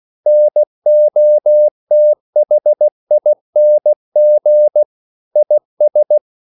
Morse code, heard as a keyed tone, about 16 wpm, 600 Hz. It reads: NOTHING IS
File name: CW / Practice / EnglishQuotations0416